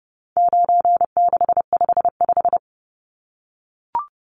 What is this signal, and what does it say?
Morse code, heard as a keyed tone, about 30 wpm, 700 Hz. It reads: 9655
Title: Numbers - 4-Digit 30wpm